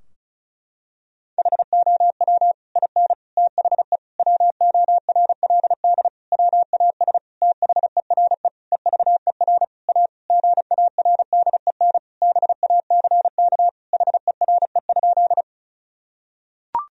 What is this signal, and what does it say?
Morse code, heard as a keyed tone, about 34 wpm, 700 Hz. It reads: HOW IN THE WORLD WAS THERE EVER A GARDEN BACK HERE?
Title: Sentences from Top 1000 Words - Review Part 4 - 35wpm